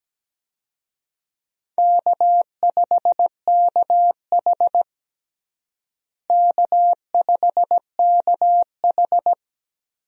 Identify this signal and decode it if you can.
Morse code, heard as a keyed tone, about 17 wpm, 700 Hz. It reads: K5KH K5KH
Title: USA Call Signs - 4 Characters - Repeated 2x - 17wpm